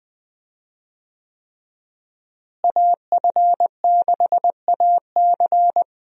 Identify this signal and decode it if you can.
Morse code, heard as a keyed tone, about 20 wpm, 700 Hz. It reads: AF6AC
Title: USA Call Signs - 5 Characters - Repeated 3x - 20wpm